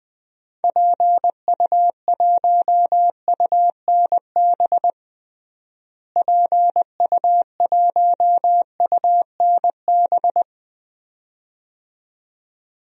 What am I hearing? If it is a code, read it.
Morse code, heard as a keyed tone, about 20 wpm, 700 Hz. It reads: PU1UNB PU1UNB
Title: DXCC-100 Call Signs - 6 Characters - Repeated 2x - 20wpm